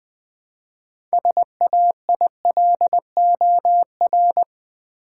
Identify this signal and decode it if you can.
Morse code, heard as a keyed tone, about 20 wpm, 700 Hz. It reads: SAILOR
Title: Build A Word - 3 Letters At A Time - 6-Letter Words - 20wpm